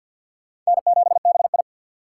Morse code, heard as a keyed tone, about 65 wpm, 700 Hz. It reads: N7BI